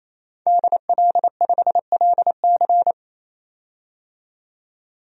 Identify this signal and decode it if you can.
Morse code, heard as a keyed tone, about 28 wpm, 700 Hz. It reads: DL5LC
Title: DXCC-100 Call Signs - 5 Characters - 28wpm